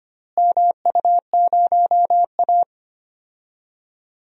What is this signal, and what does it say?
Morse code, heard as a keyed tone, about 25 wpm, 700 Hz. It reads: MU0A
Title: DXCC-100 Call Signs - 4 Characters - Repeated 3x - 25wpm